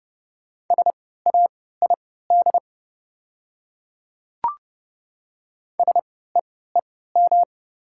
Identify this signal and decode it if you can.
Morse code, heard as a keyed tone, about 29 wpm, spaced as for 17 wpm, 690 Hz. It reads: SAID SEEM
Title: 3x Character Spacing - Top 300 Words - 30wpm